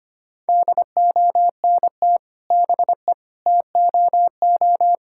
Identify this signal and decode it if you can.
Morse code, heard as a keyed tone, about 25 wpm, 700 Hz. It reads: DONT BE TOO
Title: Sentences from Top 1000 Words - Review Part 5 - 25wpm